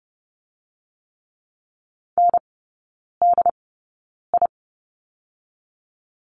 Morse code, heard as a keyed tone, about 30 wpm, 700 Hz. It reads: N D I
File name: 081 Lesson 14e - Number 2 - Three Characters 30wpm